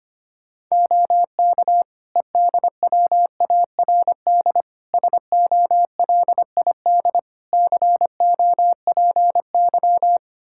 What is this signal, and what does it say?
Morse code, heard as a keyed tone, about 25 wpm, 700 Hz. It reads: OK EDWARD SOLID COPY
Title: Practice QSO - Exchange 3 - More Info - 25wpm